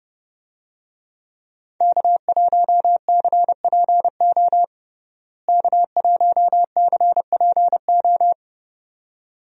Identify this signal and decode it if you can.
Morse code, heard as a keyed tone, about 32 wpm, 700 Hz. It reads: K1CPO K1CPO
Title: USA Call Signs - 5 Characters - Repeated 2x - 30wpm